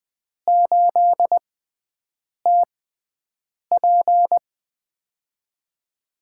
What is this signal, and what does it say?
Morse code, heard as a keyed tone, about 20 wpm, spaced as for 7 wpm, 700 Hz. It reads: 8TP
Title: USA Call Signs - 6x Character Spacing - 20wpm